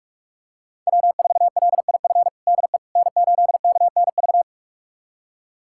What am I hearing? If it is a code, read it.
Morse code, heard as a keyed tone, about 44 wpm, 700 Hz. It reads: W4LIF DE N8KNV
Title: USA Callsign DE Callsign - 5 Characters - 45wpm